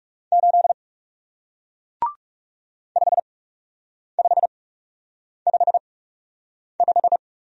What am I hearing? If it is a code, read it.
Morse code, heard as a keyed tone, about 39 wpm, spaced as for 8 wpm, 700 Hz. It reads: 8 5555